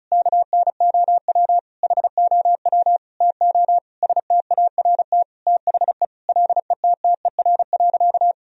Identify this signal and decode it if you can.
Morse code, heard as a keyed tone, about 35 wpm, 700 Hz. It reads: KNOW HOW TO START THE LETTER.
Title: Sentences from Top 1000 Words - Review Part 1 - 35wpm